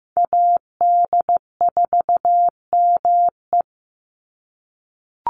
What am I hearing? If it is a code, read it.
Morse code, heard as a keyed tone, about 15 wpm, 700 Hz. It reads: AD4ME
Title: USA Call Signs 15wpm